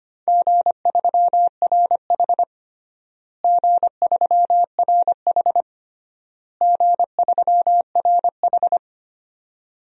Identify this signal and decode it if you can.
Morse code, heard as a keyed tone, about 25 wpm, 700 Hz. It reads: G3RH G3RH G3RH
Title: DXCC-100 Call Signs - 4 Characters - Repeated 3x - 25wpm